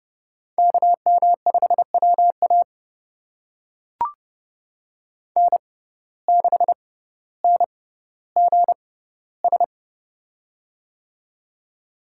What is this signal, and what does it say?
Morse code, heard as a keyed tone, about 30 wpm, spaced as for 15 wpm, 700 Hz. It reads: KM5WA N6NGS
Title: USA Call Signs - 5 Characters - 6x Character Spacing - 30wpm